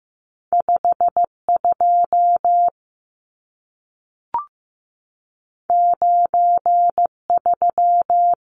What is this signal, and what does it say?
Morse code, heard as a keyed tone, about 15 wpm, 700 Hz. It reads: 52 93
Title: Numbers - 2-Digit 15wpm